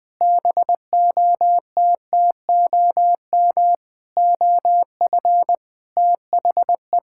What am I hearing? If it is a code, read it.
Morse code, heard as a keyed tone, about 20 wpm, 700 Hz. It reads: BOTTOM OF THE
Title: Sentences with 5-Word Phrases - 20wpm